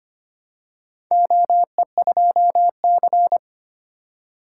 Morse code, heard as a keyed tone, about 25 wpm, 700 Hz. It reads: OE2C
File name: practice / 25wpm / Four Letter-Number Combinations - 25wpm